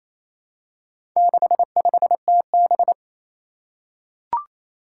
Morse code, heard as a keyed tone, about 28 wpm, 700 Hz. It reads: 65TB
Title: Four Challenging Letter-Number Combinations 28wpm